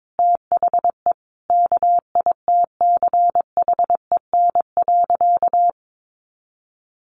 Morse code, heard as a keyed tone, about 22 wpm, 700 Hz. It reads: THE KITCHEN.
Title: Easy Phrases 22wpm